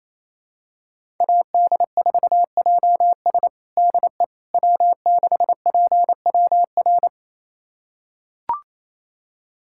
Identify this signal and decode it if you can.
Morse code, heard as a keyed tone, about 28 wpm, 700 Hz. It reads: AD4JS DE W6PWR
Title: USA Callsign DE Callsign - 5 Characters - 28wpm